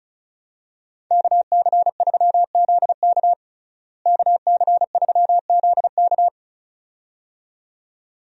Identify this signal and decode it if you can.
Morse code, heard as a keyed tone, about 34 wpm, 700 Hz. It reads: KC3ZK KC3ZK
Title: USA Call Signs - 5 Characters - Repeated 2x - 35wpm